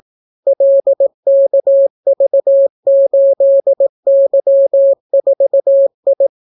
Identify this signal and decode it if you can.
Morse code, heard as a keyed tone, about 18 wpm, 550 Hz. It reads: LKV8Y4I